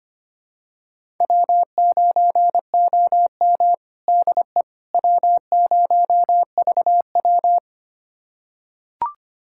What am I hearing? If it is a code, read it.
Morse code, heard as a keyed tone, about 25 wpm, 700 Hz. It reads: W9OM DE W0VW
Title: USA Callsign DE Callsign - 4 Characters - 25wpm